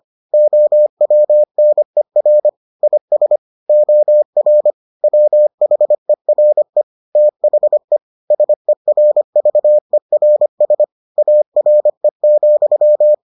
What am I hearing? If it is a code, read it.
Morse code, heard as a keyed tone, about 25 wpm, 600 Hz. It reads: OWNER IS OR WHERE THE SERVERS ARE,